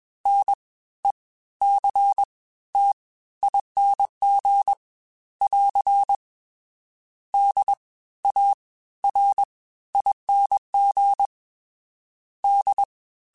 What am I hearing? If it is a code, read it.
Morse code, heard as a keyed tone, about 21 wpm, 800 Hz. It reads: N E C T ING <AR> D A R ING D
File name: Special INGs-mp3s / Special INGs-mp3s / Fast ing @20